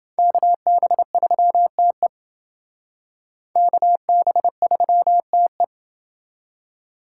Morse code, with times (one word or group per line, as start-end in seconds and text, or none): 0.2-2.1 KB3TE
3.6-5.6 KB3TE